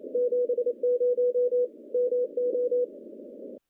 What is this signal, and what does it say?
Morse code, heard as a keyed tone, about 27 wpm, 490 Hz. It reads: 70 MO
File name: contests / CW-SS-k5tr- -0000